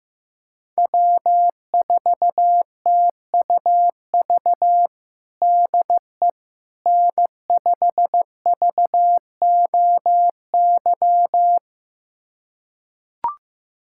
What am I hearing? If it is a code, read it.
Morse code, heard as a keyed tone, about 16 wpm, 700 Hz. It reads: W4TUV DE N5VOY
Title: USA Callsign DE Callsign - 5 Characters - 15wpm